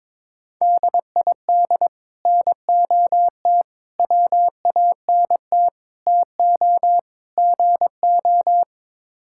Morse code, heard as a keyed tone, about 22 wpm, 700 Hz. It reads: DID NOT WANT TO GO
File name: practice / 22wpm / Build A Phrase - 5-Word Phrases - 22wpm